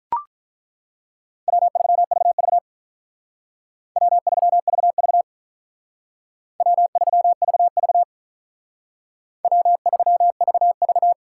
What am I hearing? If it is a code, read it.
Morse code, heard as a keyed tone, about 43 wpm, 700 Hz. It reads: W3VV W3VV W3VV W3VV
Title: Speed Racing - USA 1x2 Call Signs 35wpm